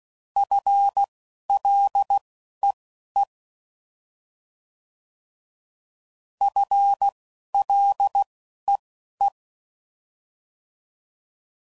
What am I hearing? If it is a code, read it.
Morse code, heard as a keyed tone, about 16 wpm, spaced as for 12 wpm, 800 Hz. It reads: FLEE FLEE